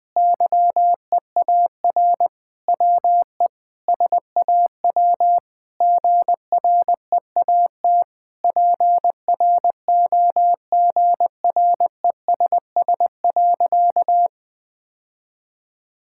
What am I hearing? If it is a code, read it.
Morse code, heard as a keyed tone, about 21 wpm, 700 Hz. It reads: YEAR WE SAW GREAT PROGRESS.
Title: Sentences with 7-Word Phrases - Spoken Beginning - 20wpm